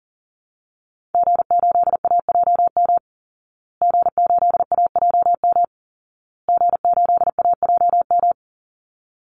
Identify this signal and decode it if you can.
Morse code, heard as a keyed tone, about 40 wpm, 700 Hz. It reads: G8AJM G8AJM G8AJM
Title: Call Signs Repeated 2x - 40wpm